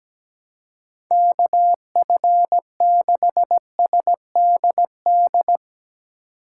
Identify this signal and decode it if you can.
Morse code, heard as a keyed tone, about 17 wpm, 700 Hz. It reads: KF6SDD
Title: USA Call Signs - 6 Characters - 6x Character Spacing - 17wpm